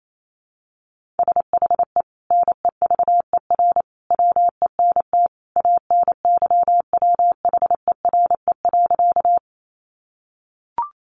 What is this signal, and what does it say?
Morse code, heard as a keyed tone, about 28 wpm, 700 Hz. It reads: SHE NEVER WENT ANYWHERE.
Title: Sentences from Top 1400 Words 28wpm